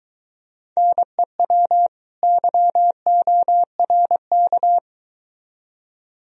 Morse code, heard as a keyed tone, about 22 wpm, 700 Hz. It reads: NEW YORK